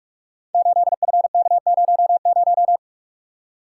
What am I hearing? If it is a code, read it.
Morse code, heard as a keyed tone, about 44 wpm, 700 Hz. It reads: 8FK00
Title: Code Groups - 5 Letter:Numbers - 45wpm